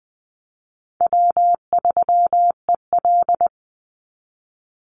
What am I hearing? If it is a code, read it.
Morse code, heard as a keyed tone, about 20 wpm, 700 Hz. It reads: W3EL
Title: Call Signs Repeated 2x - 20wpm